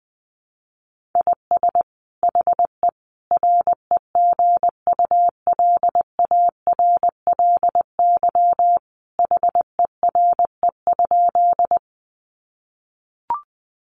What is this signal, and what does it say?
Morse code, heard as a keyed tone, about 20 wpm, 700 Hz. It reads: IS HE REGULARLY HERE?
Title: Sentences from Top 1400 Words 20wpm